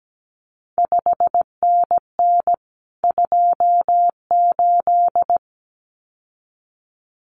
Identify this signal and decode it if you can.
Morse code, heard as a keyed tone, about 17 wpm, 700 Hz. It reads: 5NN 28
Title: CQ WW DX Contest Exchange Encore - 17wpm